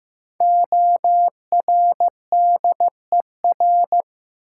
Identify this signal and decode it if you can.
Morse code, heard as a keyed tone, about 15 wpm, 700 Hz. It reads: ORDER
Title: Sentences from Top 400 Words - New Word Drill - 15wpm